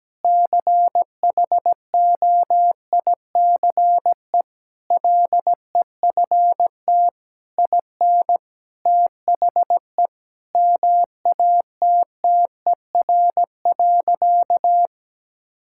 Morse code, heard as a keyed tone, about 17 wpm, 700 Hz. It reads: CHOICE LEFT IN THE MATTER.
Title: Sentences from Top 1000 Words - Review Part 6 - 17wpm